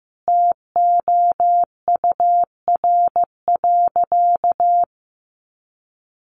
Morse code, heard as a keyed tone, about 15 wpm, 700 Hz. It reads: TOUR.
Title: Sentences from Top 1400 Words 15wpm